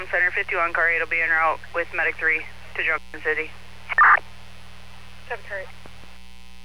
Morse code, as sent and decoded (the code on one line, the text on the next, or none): none